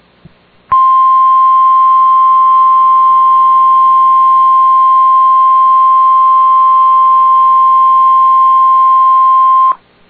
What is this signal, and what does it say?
Emergency Alert System (SAME) tones (the 1050 Hz attention tone). It is heard from 0.7 to 9.8 s.